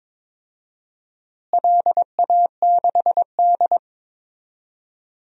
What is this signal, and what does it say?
Morse code, heard as a keyed tone, about 22 wpm, 700 Hz. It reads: LA6D
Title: DXCC-100 Call Signs - 4 Characters - Repeated 2x - 22wpm